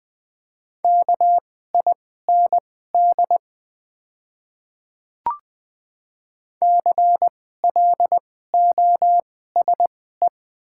Morse code, heard as a keyed tone, about 20 wpm, spaced as for 14 wpm, 690 Hz. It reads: KIND CLOSE
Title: 2x Character Spacing - Top 300 Words - 20wpm